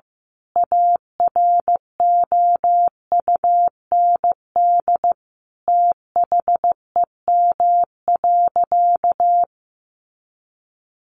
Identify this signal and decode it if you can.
Morse code, heard as a keyed tone, about 15 wpm, 700 Hz. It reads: AROUND THEM.
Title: A Sound Of Thunder 15wpm